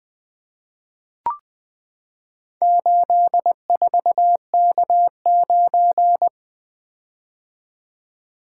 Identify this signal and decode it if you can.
Morse code, heard as a keyed tone, about 20 wpm, 700 Hz. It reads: 84K9
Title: Four Letter-Number Combinations - 20wpm